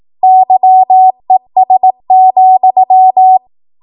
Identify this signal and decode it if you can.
Morse code, heard as a keyed tone, about 18 wpm, 750 Hz. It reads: YES,
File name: Morse / 18WPM